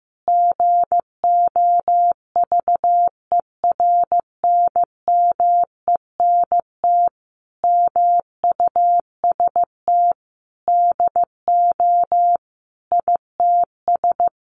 Morse code, heard as a keyed tone, about 15 wpm, 700 Hz. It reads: GOVERNMENT MUST DO ITS
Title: Sentences from Top 1100 Words 15wpm